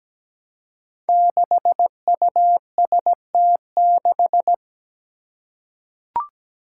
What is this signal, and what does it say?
Morse code, heard as a keyed tone, about 17 wpm, 700 Hz. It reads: 6UST6
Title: Five Challenging Letter-Number Combinations 17wpm